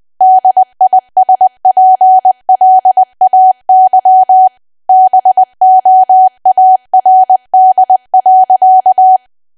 Morse code, heard as a keyed tone, about 20 wpm, 750 Hz. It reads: DISPLAY BOARD.
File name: cwa / W1AW-Level3 / 20WPM